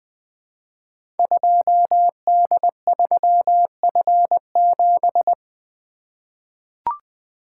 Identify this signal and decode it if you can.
Morse code, heard as a keyed tone, about 20 wpm, 700 Hz. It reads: 2D3F7
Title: Code Groups - 5 Letter:Numbers - 20wpm